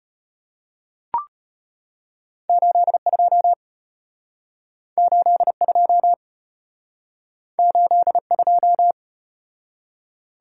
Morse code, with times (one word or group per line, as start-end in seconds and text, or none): 2.5-3.5 82
5.0-6.1 82
7.6-8.9 82